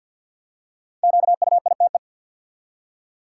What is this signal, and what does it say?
Morse code, heard as a keyed tone, about 53 wpm, 700 Hz. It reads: QUITE
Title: Five Letter Words - 50wpm